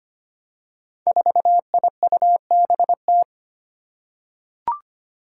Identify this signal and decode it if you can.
Morse code, heard as a keyed tone, about 25 wpm, 700 Hz. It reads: 4IUBT
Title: Five Challenging Letter-Number Combinations 25wpm